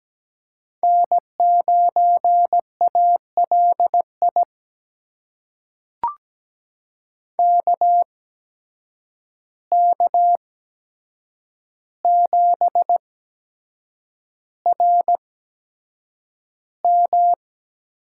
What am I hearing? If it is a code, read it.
Morse code, heard as a keyed tone, about 17 wpm, spaced as for 7 wpm, 700 Hz. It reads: N9ALI KK7RM
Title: USA Call Signs - 8x Character Spacing - 17wpm